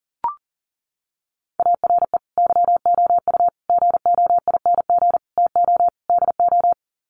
Morse code, heard as a keyed tone, about 40 wpm, 700 Hz. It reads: ARE YOU GOING TO DO
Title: Sets of 5 Words 40wpm